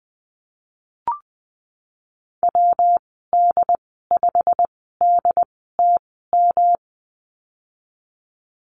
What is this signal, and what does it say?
Morse code, heard as a keyed tone, about 20 wpm, spaced as for 15 wpm, 700 Hz. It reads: WD5DTM
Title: 122 Lesson 20g - Number 3 - Call Signs 20wpm